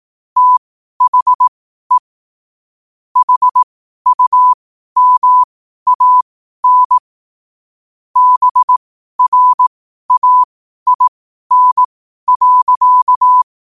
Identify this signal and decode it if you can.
Morse code, heard as a keyed tone, about 18 wpm, spaced as for 12 wpm, 1000 Hz. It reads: THE HUMAN BRAIN.
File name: qotdfiles / 12wpm